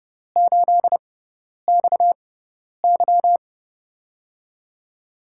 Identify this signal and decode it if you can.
Morse code, heard as a keyed tone, about 31 wpm, spaced as for 10 wpm, 700 Hz. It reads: 8XY
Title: USA Call Signs - 6x Character Spacing - 30wpm